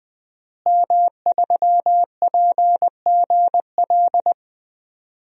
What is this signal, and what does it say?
Morse code, heard as a keyed tone, about 20 wpm, 700 Hz. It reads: M3PGL
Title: DXCC-100 Call Signs - 5 Characters - Repeated 2x - 20wpm